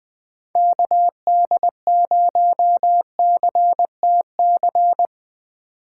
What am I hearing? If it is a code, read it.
Morse code, heard as a keyed tone, about 21 wpm, 700 Hz. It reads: KD0CTC
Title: Speed Racing - USA 2x3 Call Signs 17wpm